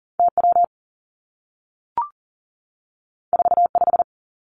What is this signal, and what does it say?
Morse code, heard as a keyed tone, about 39 wpm, 700 Hz. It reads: T W 4 5
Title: Two Challenging Letter-Number Combinations 40wpm